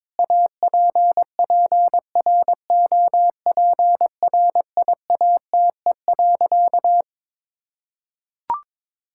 Morse code, heard as a keyed tone, about 22 wpm, 700 Hz. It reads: APPROPRIATE.
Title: Sentences from Top 1000 Words - Review Part 5 - 22wpm